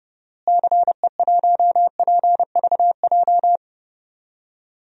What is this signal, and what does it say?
Morse code, heard as a keyed tone, about 30 wpm, 700 Hz. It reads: CE1PVJ
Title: DXCC-100 Call Signs - 6 Characters - Repeated 3x - 30wpm